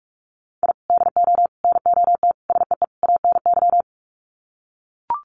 Morse code, heard as a keyed tone, about 44 wpm, 700 Hz. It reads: I DO NOT SEE ANY
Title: Sets of 5 Words 45wpm